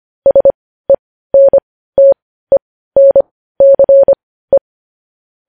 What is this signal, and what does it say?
Morse code, heard as a keyed tone, about 25 wpm, spaced as for 15 wpm, 550 Hz. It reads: SENTENCE
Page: news-headlines-in-morse-code-at-15-wpm